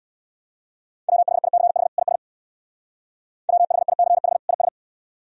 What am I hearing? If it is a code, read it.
Morse code, heard as a keyed tone, about 63 wpm, 700 Hz. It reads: WHICH IS WHICH IS